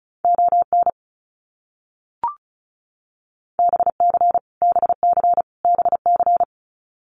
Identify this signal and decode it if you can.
Morse code, heard as a keyed tone, about 34 wpm, 700 Hz. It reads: ON BC BC BC
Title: Canadian Provinces and Territories - Repeated 3x - 35wpm